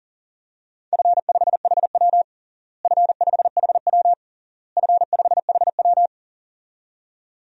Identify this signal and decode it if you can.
Morse code, heard as a keyed tone, about 37 wpm, 700 Hz. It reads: F5HW F5HW F5HW
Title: DXCC-100 Call Signs - 4 Characters - Repeated 3x - 40wpm